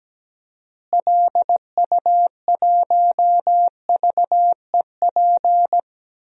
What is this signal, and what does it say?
Morse code, heard as a keyed tone, about 17 wpm, 700 Hz. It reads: LU1VEP
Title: DXCC-100 Call Signs - 6 Characters - Repeated 2x - 17wpm